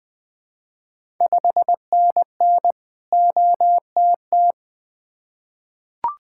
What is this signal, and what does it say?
Morse code, heard as a keyed tone, about 20 wpm, 700 Hz. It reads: 5NN OTT
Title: Ontario QSO Party Exchange 20wpm